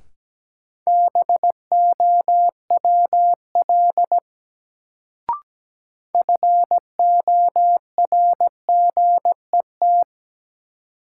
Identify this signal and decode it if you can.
Morse code, heard as a keyed tone, about 17 wpm, 700 Hz. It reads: BOWL FORGET
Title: Sentences from Top 1200 Words - New Word Drill - 17wpm